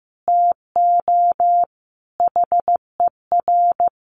Morse code, heard as a keyed tone, about 15 wpm, 700 Hz. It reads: TO HER
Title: Sentences from Top 1500 Words 15wpm